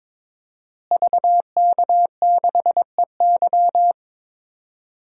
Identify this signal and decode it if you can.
Morse code, heard as a keyed tone, about 22 wpm, 700 Hz. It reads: VK6EY
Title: DXCC-100 Call Signs - 5 Characters - Repeated 2x - 22wpm